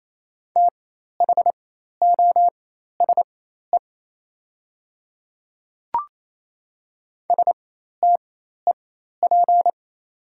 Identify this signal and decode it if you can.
Morse code, heard as a keyed tone, about 28 wpm, spaced as for 13 wpm, 700 Hz. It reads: THOSE STEP
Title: 4x Character Spacing - Top 500 Words - 28wpm